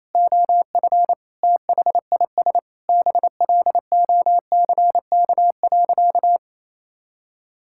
Morse code, heard as a keyed tone, about 27 wpm, 700 Hz. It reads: OF THIS BLOCK.
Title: Sentences from Top 1600 Words 28wpm